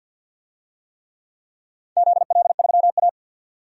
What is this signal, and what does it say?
Morse code, heard as a keyed tone, about 51 wpm, 700 Hz. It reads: ZL3A
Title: DXCC-100 Call Signs - 4 Characters - 50wpm